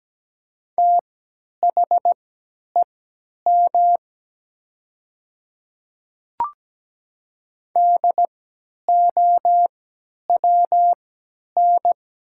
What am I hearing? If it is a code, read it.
Morse code, heard as a keyed tone, about 17 wpm, spaced as for 10 wpm, 700 Hz. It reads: THEM DOWN
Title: 3x Character Spacing - Top 100 Words - 17wpm